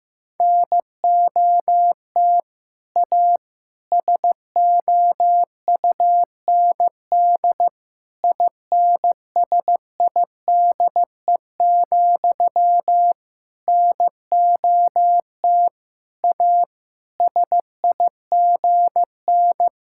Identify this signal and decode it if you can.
Morse code, heard as a keyed tone, about 15 wpm, 700 Hz. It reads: NOT A SOUND INSIDE, NOT A SIGN
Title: Sentences from Top 1000 Words - Review Part 6 - 15wpm